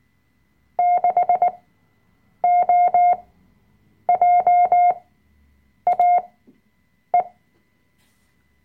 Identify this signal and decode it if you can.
Morse code, heard as a keyed tone, about 19 wpm, spaced as for 7 wpm, 710 Hz. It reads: 6OJAE
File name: Workshop Dateien / CW-Online-Kurs / Lektion 005